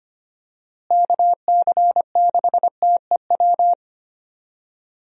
Morse code, heard as a keyed tone, about 25 wpm, 700 Hz. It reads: KC6TEW
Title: USA Call Signs - 6 Characters - Repeated 3x - 25wpm